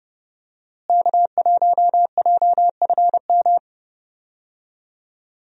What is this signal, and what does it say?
Morse code, heard as a keyed tone, about 32 wpm, 700 Hz. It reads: K1JFM
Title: USA Call Signs - 5 Characters - Repeated 2x - 30wpm